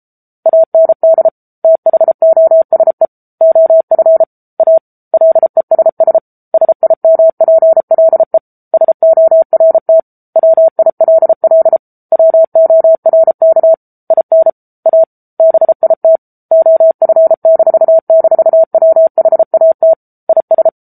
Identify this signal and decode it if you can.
Morse code, heard as a keyed tone, about 34 wpm, 650 Hz. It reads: AND THOSE OF A LESS SIMPLE SORT WILL WORK IN A BIT OF--WHAT IS
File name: Book Two / 07-Chapter 7 - 33 WPM